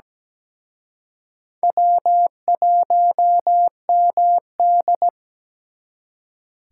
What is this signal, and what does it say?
Morse code, heard as a keyed tone, about 21 wpm, 700 Hz. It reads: W1MD
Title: Speed Racing - USA 1x2 Call Signs 17wpm